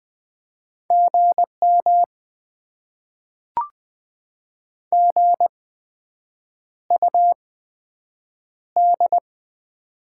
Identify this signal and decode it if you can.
Morse code, heard as a keyed tone, about 20 wpm, 700 Hz. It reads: G M G U D
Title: Top 20 QSO Elements - 8x Character Spacing - 20wpm